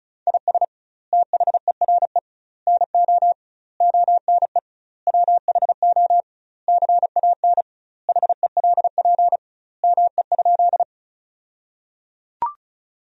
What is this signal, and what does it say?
Morse code, heard as a keyed tone, about 34 wpm, 700 Hz. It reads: IS THERE NO ONE WHO CAN HELP ME?
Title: 2x Word-Spacing - Sentences from Top 200 Words - 35wpm